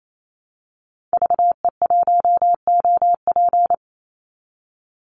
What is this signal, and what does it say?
Morse code, heard as a keyed tone, about 28 wpm, 700 Hz. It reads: VE1OP
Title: Call Signs Repeated 2x - 28wpm